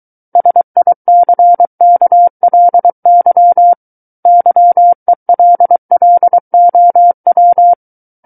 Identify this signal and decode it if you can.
Morse code, heard as a keyed tone, about 23 wpm, 700 Hz. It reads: SICKLY YELLOW
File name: Book One / 14-Chapter 14 - 23 WPM